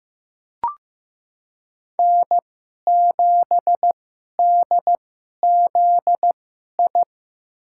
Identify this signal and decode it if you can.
Morse code, heard as a keyed tone, about 15 wpm, spaced as for 11 wpm, 700 Hz. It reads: N7DZI
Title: USA Call Signs - 5 Characters - 2x Character Spacing - 15wpm